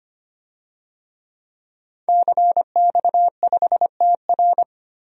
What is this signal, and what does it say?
Morse code, heard as a keyed tone, about 25 wpm, 700 Hz. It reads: CX5TR
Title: DXCC-100 Call Signs - 5 Characters - Repeated 3x - 25wpm